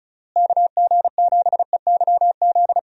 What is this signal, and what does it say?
Morse code, heard as a keyed tone, about 34 wpm, 700 Hz. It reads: KG7EYZ
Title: USA Call Signs - 6 Characters - Repeated 3x - 35wpm